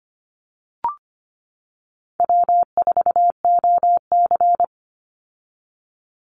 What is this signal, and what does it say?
Morse code, heard as a keyed tone, about 25 wpm, 700 Hz. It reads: W4OC
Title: Easy Call Signs - 25wpm